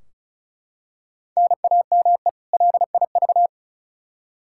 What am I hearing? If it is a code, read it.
Morse code, heard as a keyed tone, about 34 wpm, 700 Hz. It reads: NAME LIV
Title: Practice QSO - Exchange 1 - Beginning - 35wpm